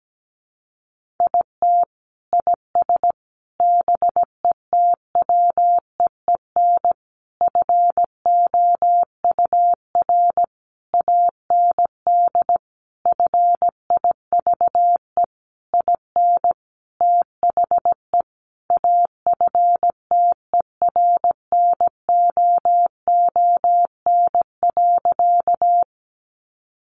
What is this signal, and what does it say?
Morse code, heard as a keyed tone, about 17 wpm, 700 Hz. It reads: IT IS BETWEEN FOUR AND FIVE IN THE AFTERNOON.
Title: Sentences from Top 1500 Words 17wpm